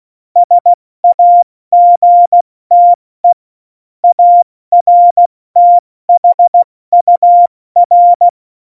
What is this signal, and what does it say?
Morse code, heard as a keyed tone, about 16 wpm, 700 Hz. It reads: SAGTE ARTHUR
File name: Morsekurs / Witze / Witz 26. Januar 2021, 70 F80